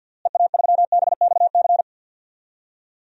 Morse code, heard as a keyed tone, about 51 wpm, 700 Hz. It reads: EA3BXC
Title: DXCC-100 Call Signs - 6 Characters - Repeated 3x - 50wpm